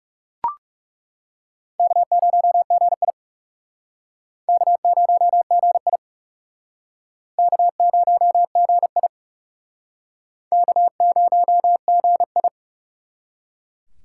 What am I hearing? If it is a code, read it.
Morse code, heard as a keyed tone, about 37 wpm, 700 Hz. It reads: K0GI K0GI K0GI K0GI
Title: Speed Racing - USA 1x2 Call Signs 30wpm